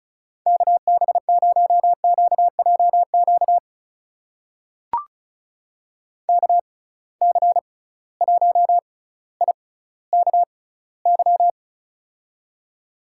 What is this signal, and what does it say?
Morse code, heard as a keyed tone, about 36 wpm, spaced as for 18 wpm, 700 Hz. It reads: KB0QJQ KC1IKY
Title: USA Call Signs - 6x Character Spacing - 35wpm